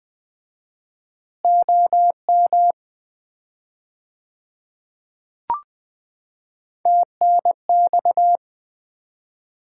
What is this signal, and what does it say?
Morse code, heard as a keyed tone, about 20 wpm, 700 Hz. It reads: OM TNX
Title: Common QSO Phrases - 4x Word-Spacing - 20wpm